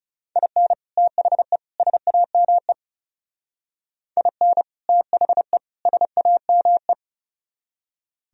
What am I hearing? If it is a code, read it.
Morse code, heard as a keyed tone, about 38 wpm, 700 Hz. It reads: IN THE SAME IN THE SAME
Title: Speed-Racing - Sets of 3 Words 30wpm